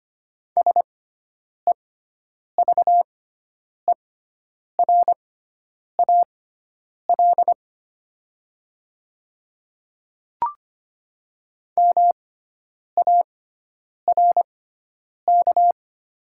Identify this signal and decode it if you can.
Morse code, heard as a keyed tone, about 25 wpm, spaced as for 9 wpm, 700 Hz. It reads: SEVERAL MARK
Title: 6x Character Spacing - Top 500 Words - 25wpm